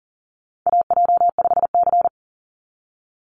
Morse code, heard as a keyed tone, about 40 wpm, 700 Hz. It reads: AJ5C
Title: NA SOTA Chaser Callsigns 40wpm